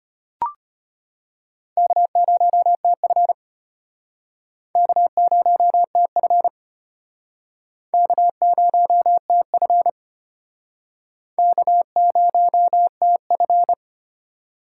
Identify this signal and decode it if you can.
Morse code, heard as a keyed tone, about 34 wpm, 700 Hz. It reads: K0TF K0TF K0TF K0TF